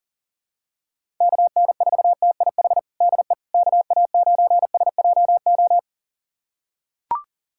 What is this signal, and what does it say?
Morse code, heard as a keyed tone, about 40 wpm, 700 Hz. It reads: KN4TIH DE KA9SJO
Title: USA Callsign DE Callsign - 6 Characters - 40wpm